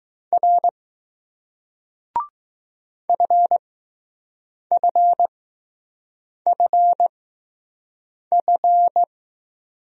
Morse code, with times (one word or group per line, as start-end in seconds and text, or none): 0.3-0.7 R
3.1-9.0 FFFF